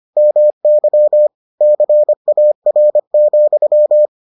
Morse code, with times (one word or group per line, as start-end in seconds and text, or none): 0.2-1.3 MY
1.6-4.1 CAR,